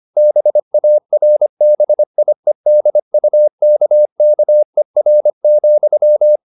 Morse code, heard as a keyed tone, about 25 wpm, 600 Hz. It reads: BARBIEDUKKER,
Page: Ferie 25 wpm